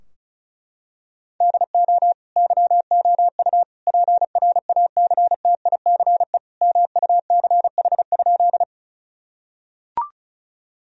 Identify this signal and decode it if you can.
Morse code, heard as a keyed tone, about 34 wpm, 700 Hz. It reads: DO YOU PRACTICE MUCH?
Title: Sentences from Top 1000 Words - Review Part 4 - 35wpm